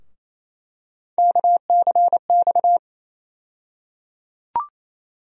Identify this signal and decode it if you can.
Morse code, heard as a keyed tone, about 28 wpm, 700 Hz. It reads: KCX
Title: USA Call Sign Postfixes - 3 Letters 28wpm